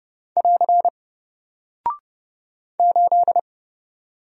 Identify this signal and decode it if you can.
Morse code, heard as a keyed tone, about 30 wpm, 700 Hz. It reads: <AR> 8
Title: Single Character (Letter:Number:Punctuation:Prosign) Alt 30wpm